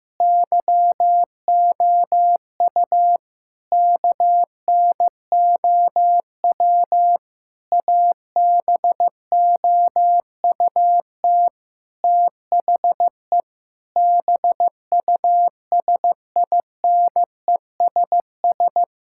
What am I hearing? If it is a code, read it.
Morse code, heard as a keyed tone, about 16 wpm, 700 Hz. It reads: YOU KNOW ABOUT THE BUSINESS
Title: Sentences from Top 1000 Words - Review Part 6 - 15wpm